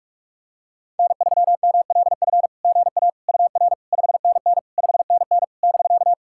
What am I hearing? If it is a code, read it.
Morse code, heard as a keyed tone, about 44 wpm, 660 Hz. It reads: N3MLF GA UR 5NN 5NN <BK>
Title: SOTA QSO Practice 45wpm